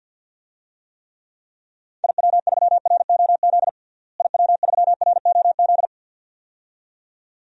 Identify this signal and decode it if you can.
Morse code, heard as a keyed tone, about 49 wpm, 700 Hz. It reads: IW3ROZ IW3ROZ
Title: DXCC-100 Call Signs - 6 Characters - Repeated 2x - 50wpm